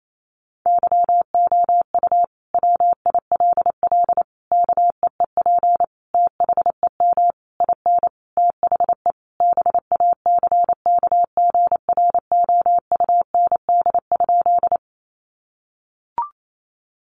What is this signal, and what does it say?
Morse code, heard as a keyed tone, about 28 wpm, 700 Hz. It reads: YOU WILL KEEP THEM IN THE BACKGROUND?
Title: Sentences from Top 1300 Words 28wpm